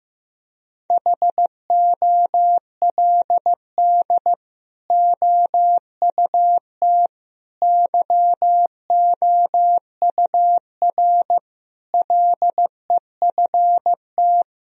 Morse code, heard as a keyed tone, about 16 wpm, 700 Hz. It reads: HOLD OUT YOUR LEFT
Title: Sentences from Top 1000 Words - Review Part 2 - 15wpm